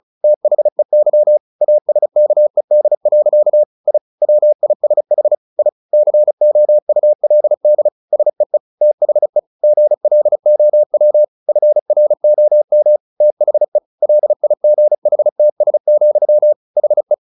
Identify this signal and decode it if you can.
Morse code, heard as a keyed tone, about 34 wpm, 600 Hz. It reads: THEY ASKED. I WISH I COULD SEE THE GLOW FROM THE LIGHTS, HE
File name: the old man and the sea 2 35wpm